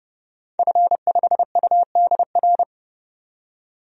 Morse code, heard as a keyed tone, about 30 wpm, 700 Hz. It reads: F5UDR
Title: DXCC-100 Call Signs - 5 Characters - 30wpm